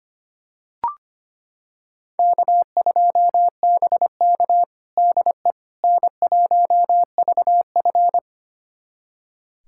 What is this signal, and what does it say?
Morse code, heard as a keyed tone, about 25 wpm, 700 Hz. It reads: K2BK DE N1VF
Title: USA Callsign DE Callsign - 4 Characters - 25wpm